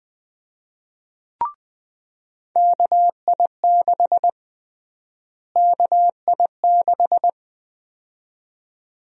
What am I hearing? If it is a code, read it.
Morse code, heard as a keyed tone, about 20 wpm, 700 Hz. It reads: KI6 KI6
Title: USA Call Sign Prefixes - Repeated 2x - 20wpm